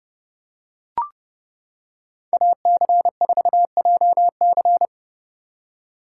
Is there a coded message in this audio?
Yes: Morse code from 2.3 to 4.9 s, heard as a keyed tone, about 29 wpm, 700 Hz. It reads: AC4JC